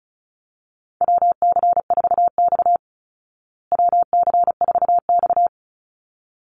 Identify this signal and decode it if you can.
Morse code, heard as a keyed tone, about 34 wpm, 700 Hz. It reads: WC4X WC4X
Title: Call Signs Repeated 3x - 35wpm